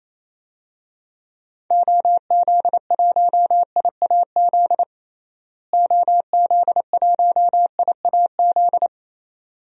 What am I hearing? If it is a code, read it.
Morse code, heard as a keyed tone, about 27 wpm, 700 Hz. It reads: OZ1IAZ OZ1IAZ
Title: DXCC-100 Call Signs - 6 Characters - Repeated 2x - 28wpm